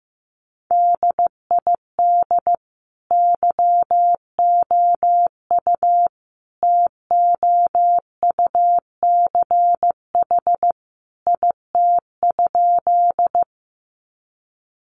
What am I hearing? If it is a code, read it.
Morse code, heard as a keyed tone, about 15 wpm, 700 Hz. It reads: DID YOU TOUCH IT?
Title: Sentences from Top 1200 Words 15wpm